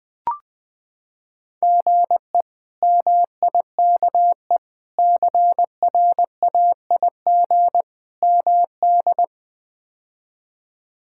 Morse code, heard as a keyed tone, about 20 wpm, 700 Hz. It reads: GE MIKE CRAIG MD
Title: K1USN SST QSO Practice - 20wpm